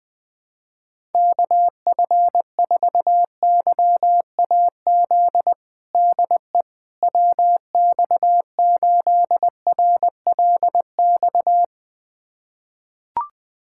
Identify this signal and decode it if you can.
Morse code, heard as a keyed tone, about 20 wpm, 700 Hz. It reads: KF4YAZ DE WX8RLX
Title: USA Callsign DE Callsign 20wpm